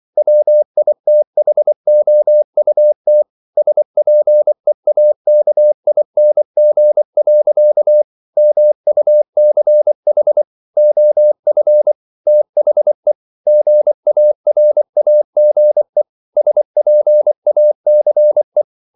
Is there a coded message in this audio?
Yes: Morse code from 0.2 to 18.6 s, heard as a keyed tone, about 24 wpm, 600 Hz. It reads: WITHOUT SPEAKING. MUCH OF THE GARAGE SPACE